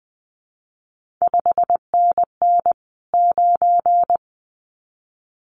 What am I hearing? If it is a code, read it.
Morse code, heard as a keyed tone, about 20 wpm, 700 Hz. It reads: 5NN 9
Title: CQ WW DX Contest Exchange Encore - 20wpm